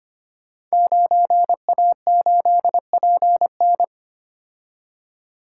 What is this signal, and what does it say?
Morse code, heard as a keyed tone, about 25 wpm, 700 Hz. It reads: 9A8PN
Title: Code Groups - 5 Letter:Numbers - 25wpm